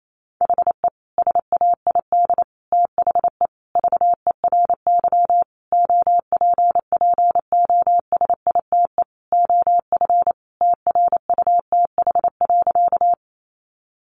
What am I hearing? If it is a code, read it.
Morse code, heard as a keyed tone, about 28 wpm, 700 Hz. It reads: HE SAID THE VERY OPPOSITE OF TRUTH.
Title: Sentences from Top 1300 Words 28wpm